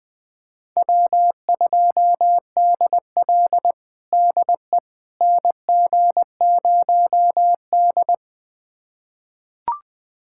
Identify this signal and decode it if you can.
Morse code, heard as a keyed tone, about 20 wpm, 700 Hz. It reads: W2DL DE NG0D
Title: USA Callsign DE Callsign - 4 Characters - 20wpm